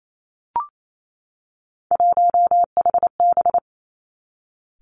Morse code, heard as a keyed tone, about 28 wpm, 700 Hz. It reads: 1HB
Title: Three Challenging Letter-Number Combinations 28wpm